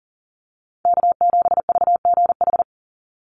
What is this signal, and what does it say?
Morse code, heard as a keyed tone, about 40 wpm, 700 Hz. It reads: K7VGH
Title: USA Call Signs 40wpm